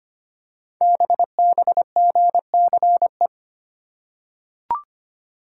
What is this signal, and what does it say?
Morse code, heard as a keyed tone, about 25 wpm, 700 Hz. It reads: BBGCE
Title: Code Groups - 5 Letter:Numbers - 25wpm